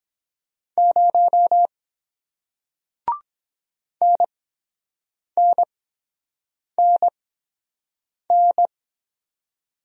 Morse code, heard as a keyed tone, about 23 wpm, spaced as for 7 wpm, 700 Hz. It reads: 0 NNNN